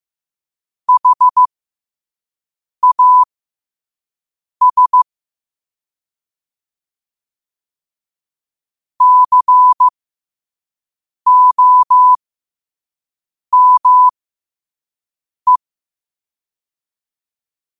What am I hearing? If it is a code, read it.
Morse code, heard as a keyed tone, about 15 wpm, spaced as for 5 wpm, 1000 Hz. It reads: HAS COME
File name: 2026-02-15-5wpm